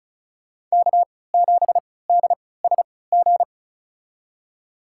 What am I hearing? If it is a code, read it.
Morse code, heard as a keyed tone, about 36 wpm, spaced as for 20 wpm, 700 Hz. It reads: K7DSG
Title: USA Call Signs - 3x Character Spacing - 35wpm